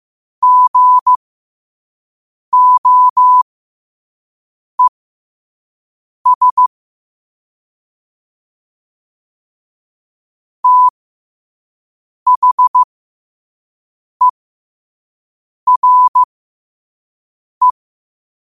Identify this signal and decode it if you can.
Morse code, heard as a keyed tone, about 15 wpm, spaced as for 5 wpm, 1000 Hz. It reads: GOES THERE
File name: qotdfiles / 5wpm